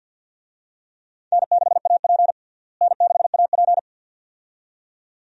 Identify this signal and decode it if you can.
Morse code, heard as a keyed tone, about 51 wpm, 700 Hz. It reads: N6AP N6AP
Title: USA Call Signs - 4 Characters - Repeated 2x - 50wpm